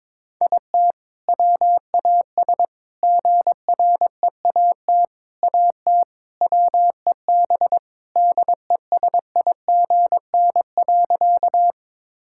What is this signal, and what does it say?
Morse code, heard as a keyed tone, about 22 wpm, 690 Hz. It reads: IT WAS GREAT AT WEB DESIGN.
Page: morse-code-ninja-practice-22wpm